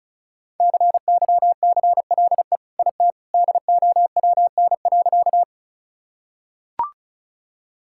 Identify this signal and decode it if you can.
Morse code, heard as a keyed tone, about 34 wpm, 700 Hz. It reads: CYCLE IT DOWN.
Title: Sentences from Top 1000 Words - Review Part 6 - 35wpm